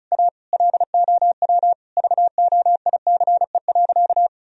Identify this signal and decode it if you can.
Morse code, heard as a keyed tone, about 36 wpm, 700 Hz. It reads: A LOW VOICE.
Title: Sentences from Top 1000 Words - Review Part 1 - 35wpm